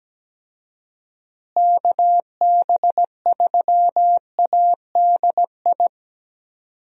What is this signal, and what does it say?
Morse code, heard as a keyed tone, about 17 wpm, 700 Hz. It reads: KB3ADI
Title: USA Call Signs - 6 Characters - Repeated 2x - 17wpm